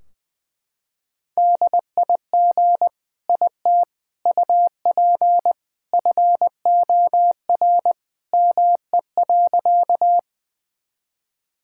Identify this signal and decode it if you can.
Morse code, heard as a keyed tone, about 20 wpm, 700 Hz. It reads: DIG IT UP FOR ME.
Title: Sentences from Top 1600 Words 20wpm